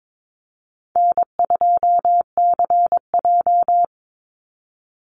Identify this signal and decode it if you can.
Morse code, heard as a keyed tone, about 22 wpm, 700 Hz. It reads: N2CJ
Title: Easy Call Signs - 22wpm